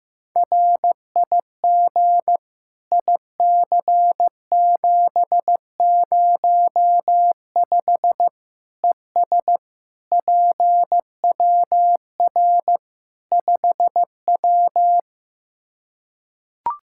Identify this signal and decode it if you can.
Morse code, heard as a keyed tone, about 15 wpm, 700 Hz. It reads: RIG IC705 ES PWR 5W
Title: Practice QSO - Exchange 2 - Info - 15wpm